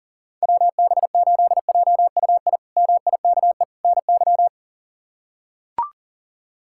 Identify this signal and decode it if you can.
Morse code, heard as a keyed tone, about 40 wpm, 700 Hz. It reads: WB8JUI MIKE NY
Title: K1USN SST QSO Practice - 40wpm